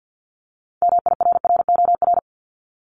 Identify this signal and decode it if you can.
Morse code, heard as a keyed tone, about 50 wpm, 700 Hz. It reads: MIRROR
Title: Multi-letter Phonemes Repeated 3x 50wpm